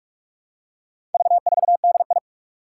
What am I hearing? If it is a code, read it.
Morse code, heard as a keyed tone, about 46 wpm, 700 Hz. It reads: V4DI
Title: Four Challenging Letter-Number Combinations 45wpm